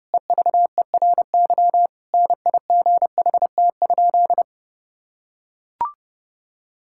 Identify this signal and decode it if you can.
Morse code, heard as a keyed tone, about 32 wpm, 700 Hz. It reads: EVERY NIGHT?
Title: Sentences from Top 1000 Words - Review Part 5 - 30wpm